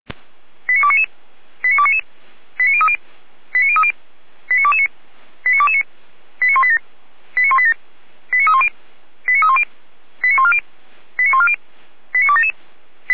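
Five-tone selective call (ZVEI), heard as quick runs of five tones, tones of about 70 ms. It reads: 89299 89299 89030 89030 89229 89229 89187 89187 89310 89310 89140 89140 89277
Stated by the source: Five-tone ZVEI selective call received by SDR, from the audio server of Feuerwehr Orlamünde (fire brigade).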